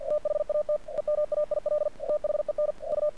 Morse code, heard as a keyed tone, about 48 wpm, 600 Hz. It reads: THAT EMAIL THEN U
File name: archived / CRSnet